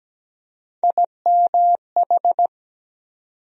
Morse code, heard as a keyed tone, about 17 wpm, 700 Hz. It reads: IMH
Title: USA Call Sign Postfixes - 3 Letters 17wpm